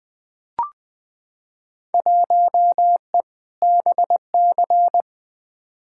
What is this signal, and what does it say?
Morse code, heard as a keyed tone, about 20 wpm, 700 Hz. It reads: 1E BC
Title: Field Day Exchanges - 20wpm